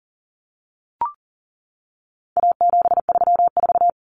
Morse code, heard as a keyed tone, about 40 wpm, 700 Hz. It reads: A734
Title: Common Cut Numbers - 4-Digit 40wpm